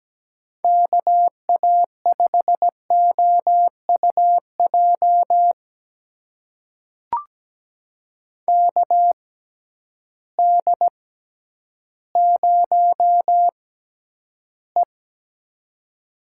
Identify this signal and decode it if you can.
Morse code, heard as a keyed tone, about 17 wpm, spaced as for 10 wpm, 700 Hz. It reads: KA5OUJ KD0E
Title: USA Call Signs - 6x Character Spacing - 17wpm